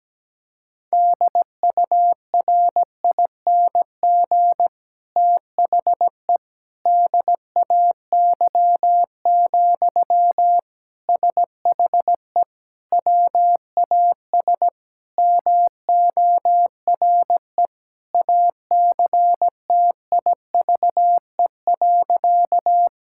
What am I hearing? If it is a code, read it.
Morse code, heard as a keyed tone, about 17 wpm, 700 Hz. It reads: DURING THE DAY, SHE WAS MORE ACTIVE.
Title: Sentences from Top 1000 Words - Review Part 6 - 17wpm